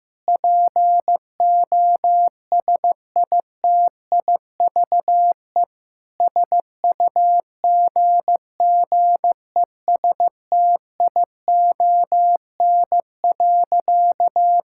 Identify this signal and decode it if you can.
Morse code, heard as a keyed tone, about 15 wpm, 700 Hz. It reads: POSITIVE SUGGESTION.
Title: Sentences from Top 1600 Words 15wpm